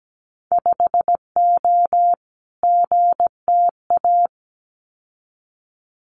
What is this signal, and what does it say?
Morse code, heard as a keyed tone, about 17 wpm, 700 Hz. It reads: 5O GTA
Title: Winter Field Day Exchanges Encore 17wpm